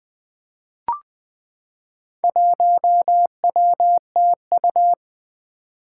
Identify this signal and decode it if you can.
Morse code, heard as a keyed tone, about 20 wpm, 700 Hz. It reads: 1WTU
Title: Four Challenging Letter-Number Combinations 20wpm